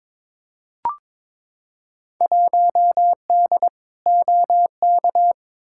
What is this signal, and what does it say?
Morse code, heard as a keyed tone, about 22 wpm, 700 Hz. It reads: 1D OK
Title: Field Day Exchanges - 22wpm